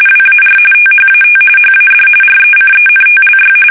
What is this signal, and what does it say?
RTTY, about 45.45 baud. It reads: WAR THAT HAD RECENTLY
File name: fsk850 / battleshipnj